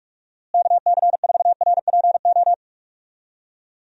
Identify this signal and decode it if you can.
Morse code, heard as a keyed tone, about 46 wpm, 700 Hz. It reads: KC4RPO
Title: USA Call Signs - 6 Characters - Repeated 3x - 45wpm